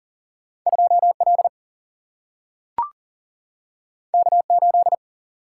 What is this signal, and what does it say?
Morse code, heard as a keyed tone, about 41 wpm, 700 Hz. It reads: 2L K8
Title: Single-Letter Single-Number 40wpm